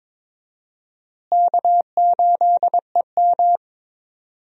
Morse code, heard as a keyed tone, about 22 wpm, 700 Hz. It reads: K8EM
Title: USA Call Signs - 4 Characters - Repeated 3x - 22wpm